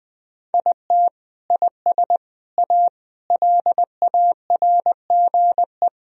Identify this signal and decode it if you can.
Morse code, heard as a keyed tone, about 20 wpm, 700 Hz. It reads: IT IS A LARGE
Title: Sentences from Top 1000 Words - Review Part 1 - 20wpm